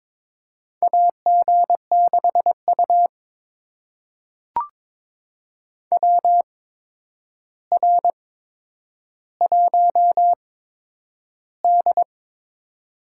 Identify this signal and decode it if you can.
Morse code, heard as a keyed tone, about 23 wpm, spaced as for 10 wpm, 700 Hz. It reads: AG6U WR1D
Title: USA Call Signs - 4 Characters - 8x Character Spacing - 22wpm